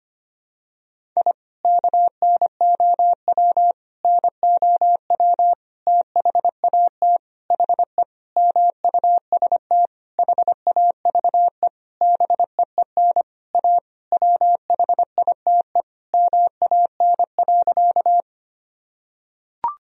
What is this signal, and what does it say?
Morse code, heard as a keyed tone, about 25 wpm, 700 Hz. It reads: I KNOW NOW THAT HE MUST HAVE BEEN A WHITE MAN.
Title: Sentences from Top 1000 Words - Review Part 1 - 25wpm